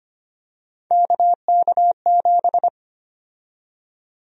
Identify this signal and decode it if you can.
Morse code, heard as a keyed tone, about 25 wpm, 700 Hz. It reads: KK7
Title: USA Call Sign Prefixes - 25wpm